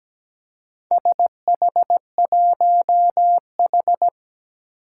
Morse code, heard as a keyed tone, about 17 wpm, 700 Hz. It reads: SH1H
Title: Four Challenging Letter-Number Combinations 17wpm